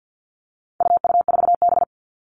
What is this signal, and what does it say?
Morse code, heard as a keyed tone, about 48 wpm, 700 Hz. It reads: UU4B
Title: All Cut Numbers - 4-Digit 50wpm